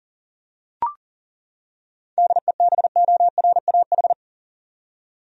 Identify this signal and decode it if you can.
Morse code, heard as a keyed tone, about 40 wpm, 700 Hz. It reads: DEBORAH
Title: USA Common Names - 40wpm